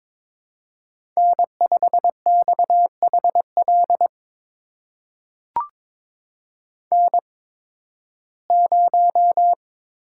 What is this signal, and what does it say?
Morse code, heard as a keyed tone, about 22 wpm, spaced as for 14 wpm, 700 Hz. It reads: N5XHL N0
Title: USA Call Signs - 5 Characters - 8x Character Spacing - 22wpm